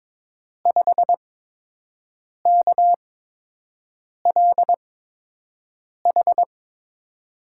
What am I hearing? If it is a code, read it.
Morse code, heard as a keyed tone, about 22 wpm, spaced as for 6 wpm, 700 Hz. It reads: 5KLH
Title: USA Call Signs - 8x Character Spacing - 22wpm